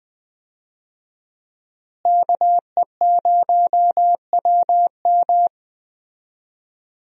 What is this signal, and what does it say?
Morse code, heard as a keyed tone, about 21 wpm, 700 Hz. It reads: KE0WM
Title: USA Call Signs - 5 Characters - Repeated 3x - 20wpm